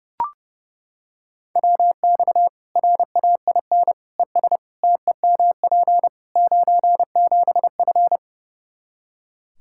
Morse code, heard as a keyed tone, about 30 wpm, 700 Hz. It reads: WX RAIN ES TEMP 97F
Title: Practice QSO - Exchange 2 - Info - 30wpm